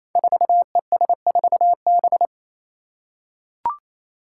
Morse code, heard as a keyed tone, about 28 wpm, 700 Hz. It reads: 4ES4B
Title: Five Challenging Letter-Number Combinations 28wpm